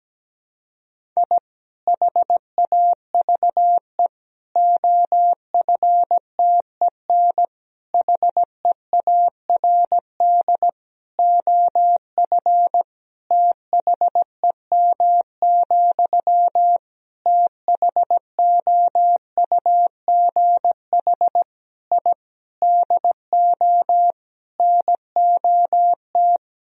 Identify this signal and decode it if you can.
Morse code, heard as a keyed tone, about 17 wpm, 700 Hz. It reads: I HAVE OFTEN HEARD OF THEM, THOUGH I DO NOT
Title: Sentences from Top 1000 Words - Review Part 2 - 17wpm